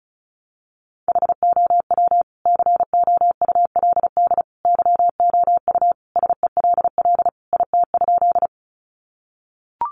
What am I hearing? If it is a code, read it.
Morse code, heard as a keyed tone, about 34 wpm, 700 Hz. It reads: HOW COULD YOU SELL IT?
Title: Sentences from Top 1100 Words 35wpm